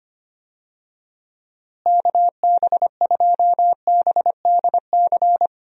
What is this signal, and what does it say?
Morse code, heard as a keyed tone, about 25 wpm, 700 Hz. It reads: KB2BDC
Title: USA Call Signs - 6 Characters - Repeated 2x - 25wpm